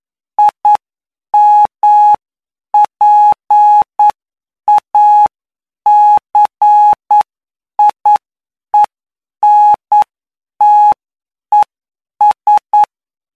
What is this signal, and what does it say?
Morse code, heard as a keyed tone, about 9 wpm, spaced as for 8 wpm, 840 Hz. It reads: IMPACIENTES
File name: cw / py2lcd / 20-PY2LCD-Li��o 29